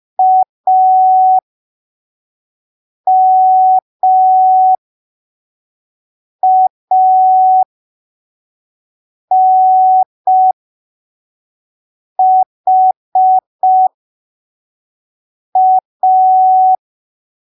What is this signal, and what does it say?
Morse code, heard as a keyed tone, about 5 wpm, spaced as for 3 wpm, 750 Hz. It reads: AMANHA
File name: arquivos / texto / texto 19 05ppm